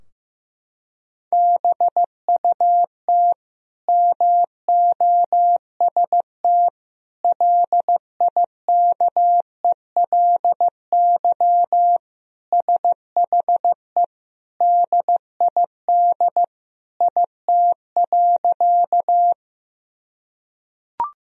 Morse code, heard as a keyed tone, about 16 wpm, 700 Hz. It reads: BUT MOST LIKELY SHE DID IT.
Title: Sentences from Top 1000 Words - Review Part 4 - 15wpm